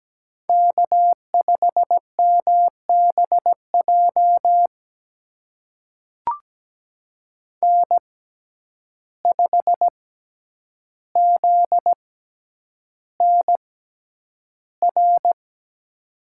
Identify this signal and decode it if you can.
Morse code, heard as a keyed tone, about 17 wpm, spaced as for 9 wpm, 700 Hz. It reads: K5MBJ N5ZNR
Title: USA Call Signs - 5 Characters - 6x Character Spacing - 17wpm